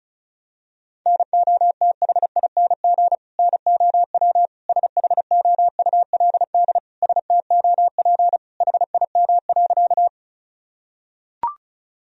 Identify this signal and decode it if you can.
Morse code, heard as a keyed tone, about 36 wpm, 700 Hz. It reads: NOTHING NOW SHOULD STOP HIM.
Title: Sentences from Top 1000 Words - Review Part 1 - 35wpm